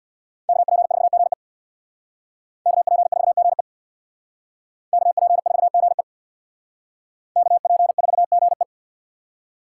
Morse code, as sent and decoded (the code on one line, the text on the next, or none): -.- .--. ....- --. / -.- .--. ....- --. / -.- .--. ....- --. / -.- .--. ....- --. .
KP4GE KP4GE KP4GE KP4GE